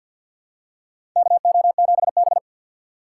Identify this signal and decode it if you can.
Morse code, heard as a keyed tone, about 46 wpm, 700 Hz. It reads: KO7B
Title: USA Call Signs - 4 Characters - 3x Character Spacing - 50wpm